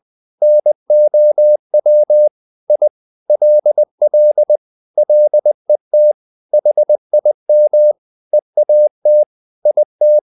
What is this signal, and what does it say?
Morse code, heard as a keyed tone, about 20 wpm, 600 Hz. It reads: NOW I LL LET HIM EAT IT
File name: the old man and the sea / the old man and the sea part 1 20wpm